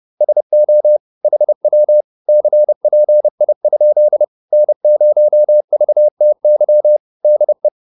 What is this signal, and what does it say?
Morse code, heard as a keyed tone, about 30 wpm, 600 Hz. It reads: SO HW CPI? N0VTY DE